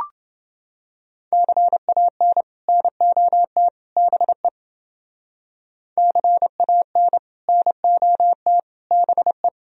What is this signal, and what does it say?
Morse code, heard as a keyed tone, about 28 wpm, 700 Hz. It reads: CAN NOT BE CAN NOT BE